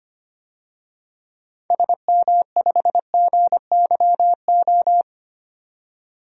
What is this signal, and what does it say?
Morse code, heard as a keyed tone, about 25 wpm, 700 Hz. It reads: SM5GYO
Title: DXCC-100 Call Signs - 6 Characters - 25wpm